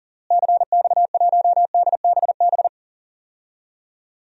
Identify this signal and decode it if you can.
Morse code, heard as a keyed tone, about 40 wpm, 700 Hz. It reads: CX1DBB
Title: DXCC-100 Call Signs - 6 Characters - Repeated 2x - 40wpm